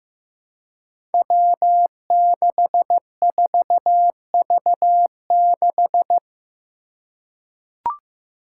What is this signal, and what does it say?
Morse code, heard as a keyed tone, about 15 wpm, 700 Hz. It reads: W64V6
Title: Five Challenging Letter-Number Combinations 15wpm